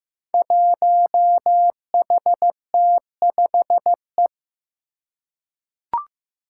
Morse code, heard as a keyed tone, about 15 wpm, 700 Hz. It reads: 1HT5E
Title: Five Challenging Letter-Number Combinations 15wpm